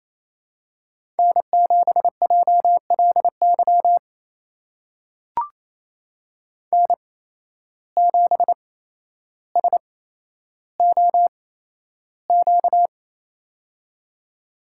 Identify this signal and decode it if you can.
Morse code, heard as a keyed tone, about 28 wpm, spaced as for 12 wpm, 700 Hz. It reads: N7JLY N7SOQ
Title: USA Call Signs - 5 Characters - 8x Character Spacing - 28wpm